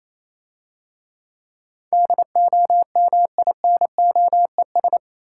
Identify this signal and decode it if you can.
Morse code, heard as a keyed tone, about 28 wpm, 700 Hz. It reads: DOMINOES